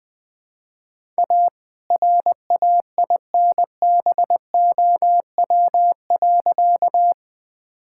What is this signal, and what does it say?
Morse code, heard as a keyed tone, about 20 wpm, 700 Hz. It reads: A RAINBOW.
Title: All-Age Appropriate Jokes - 20wpm